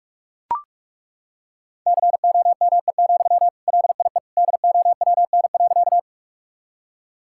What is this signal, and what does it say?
Morse code, heard as a keyed tone, about 42 wpm, 700 Hz. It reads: COME, LIE DOWN.
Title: Sentences from Top 1600 Words 45wpm